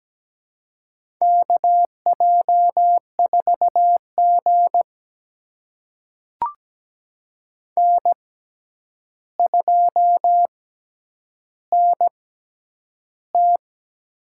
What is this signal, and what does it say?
Morse code, heard as a keyed tone, about 17 wpm, spaced as for 9 wpm, 700 Hz. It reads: KJ4G N2NT
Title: USA Call Signs - 4 Characters - 6x Character Spacing - 17wpm